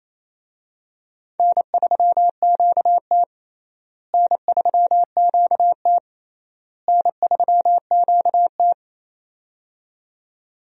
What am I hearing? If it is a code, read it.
Morse code, heard as a keyed tone, about 27 wpm, 700 Hz. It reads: N3QT N3QT N3QT
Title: USA Call Signs - 4 Characters - Repeated 3x - 28wpm